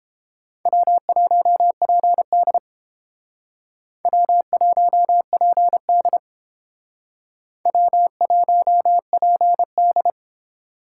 Morse code, heard as a keyed tone, about 30 wpm, 700 Hz. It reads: W1PD W1PD W1PD